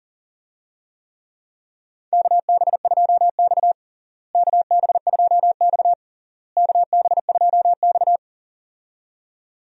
Morse code, heard as a keyed tone, about 40 wpm, 700 Hz. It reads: KB2X KB2X KB2X
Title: USA Call Signs - 4 Characters - Repeated 3x - 40wpm